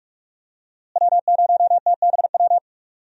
Morse code, heard as a keyed tone, about 44 wpm, 700 Hz. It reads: W0TBW
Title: USA Call Signs - 5 Characters - Repeated 3x - 45wpm